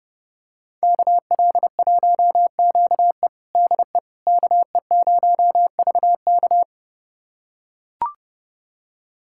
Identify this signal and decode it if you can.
Morse code, heard as a keyed tone, about 32 wpm, 700 Hz. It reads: KL1QE DE KE0VK
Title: USA Callsign DE Callsign - 5 Characters - 30wpm